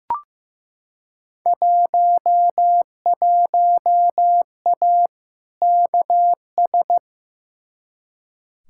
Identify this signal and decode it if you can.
Morse code, heard as a keyed tone, about 15 wpm, 700 Hz. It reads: 11A KS
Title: Field Day Exchanges Encore - 15wpm